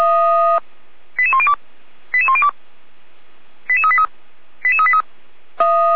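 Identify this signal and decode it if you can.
Five-tone selective call (ZVEI), heard as quick runs of five tones, tones of about 70 ms. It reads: 88102 88102 88383 88383
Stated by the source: ZVEI five-tone selective call 88467 received by SDR.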